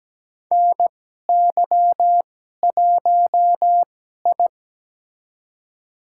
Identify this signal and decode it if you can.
Morse code, heard as a keyed tone, about 17 wpm, spaced as for 12 wpm, 700 Hz. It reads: NY1I
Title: USA Call Signs - 4 Characters - 2x Character Spacing - 17wpm